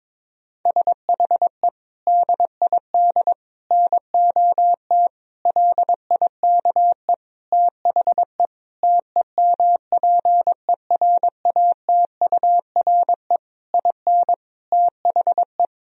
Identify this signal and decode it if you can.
Morse code, heard as a keyed tone, about 22 wpm, 700 Hz. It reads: SHE DID NOT LIKE THE TEMPERATURE IN THE
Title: Sentences from Top 1000 Words - Review Part 4 - 22wpm